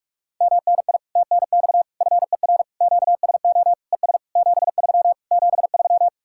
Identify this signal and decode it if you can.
Morse code, heard as a keyed tone, about 60 wpm, spaced as for 52 wpm, 700 Hz. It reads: MNI TNX FER QSO ES 73 73
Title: Speed-Racing Common QSO Phrases Encore - 45wpm